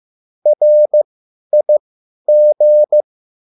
Morse code, heard as a keyed tone, about 15 wpm, spaced as for 10 wpm, 600 Hz. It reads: RIG
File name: QSL / CW lesson